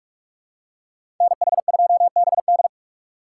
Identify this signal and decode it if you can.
Morse code, heard as a keyed tone, about 46 wpm, 700 Hz. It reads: NH2BD
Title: USA Call Signs - 5 Characters - 4x Character Spacing - 45wpm